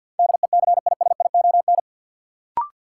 Morse code, heard as a keyed tone, about 53 wpm, 700 Hz. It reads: DECISION
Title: Sentences from Top 1000 Words - New Word Drill - 50wpm